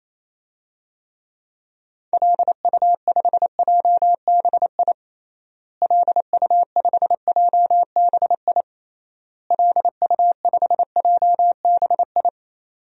Morse code, heard as a keyed tone, about 27 wpm, 700 Hz. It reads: LU5JBI LU5JBI LU5JBI
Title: DXCC-100 Call Signs - 6 Characters - Repeated 3x - 28wpm